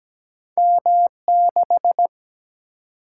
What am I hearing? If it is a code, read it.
Morse code, heard as a keyed tone, about 17 wpm, 700 Hz. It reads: M6
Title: Single-Letter Single-Number 17wpm